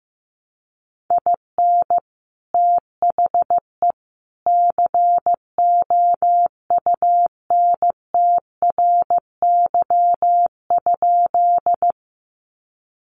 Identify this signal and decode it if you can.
Morse code, heard as a keyed tone, about 15 wpm, 700 Hz. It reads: IN THE COUNTRY?
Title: Sentences from Top 300 Words 15wpm